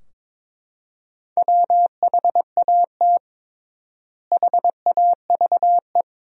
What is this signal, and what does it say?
Morse code, heard as a keyed tone, about 21 wpm, 700 Hz. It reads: WHAT HAVE
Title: 3x Word-Spacing - Sentences from Top 400 Words - 22wpm